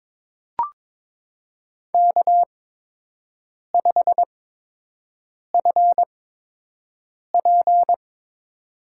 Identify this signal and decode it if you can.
Morse code, heard as a keyed tone, about 22 wpm, spaced as for 6 wpm, 700 Hz. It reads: K5FP
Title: USA Call Signs - 8x Character Spacing - 22wpm